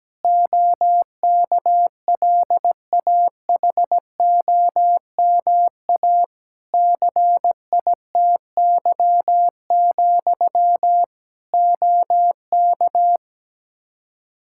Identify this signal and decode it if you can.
Morse code, heard as a keyed tone, about 17 wpm, 700 Hz. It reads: OKLAHOMA CITY, OK
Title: Practice QSO - Exchange 1 - Beginning - 17wpm